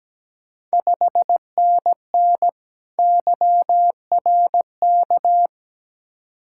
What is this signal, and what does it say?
Morse code, heard as a keyed tone, about 17 wpm, 700 Hz. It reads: 5NN YRK
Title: Ontario QSO Party Exchange 17wpm